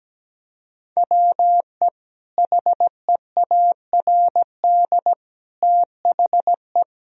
Morse code, heard as a keyed tone, about 17 wpm, 700 Hz. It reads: WE HEARD THE
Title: Sentences from Top 1000 Words - Review Part 2 - 17wpm